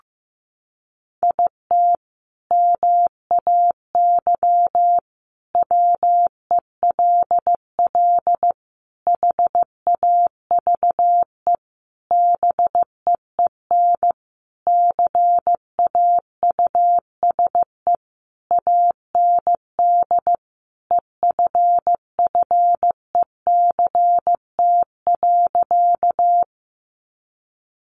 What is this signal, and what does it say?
Morse code, heard as a keyed tone, about 15 wpm, 700 Hz. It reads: IT MAY WELL HAVE BEEN CAUSE AND EFFECT.
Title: Sentences from Top 1000 Words 15wpm